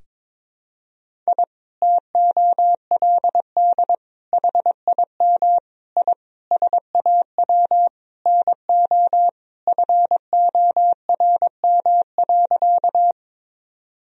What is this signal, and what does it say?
Morse code, heard as a keyed tone, about 22 wpm, 700 Hz. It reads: I TOLD HIM I SAW NO FORM.
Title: Sentences from Top 1000 Words - Review Part 1 - 22wpm